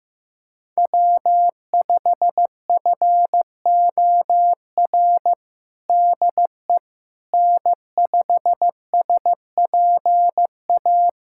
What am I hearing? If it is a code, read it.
Morse code, heard as a keyed tone, about 15 wpm, 700 Hz. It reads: W5FOR DE N5SPA
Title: USA Callsign DE Callsign - 5 Characters - 15wpm